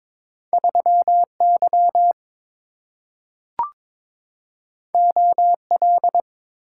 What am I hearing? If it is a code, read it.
Morse code, heard as a keyed tone, about 23 wpm, 690 Hz. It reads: 3Y OL
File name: practice / 22wpm / Two Letter-Number Combinations - 22wpm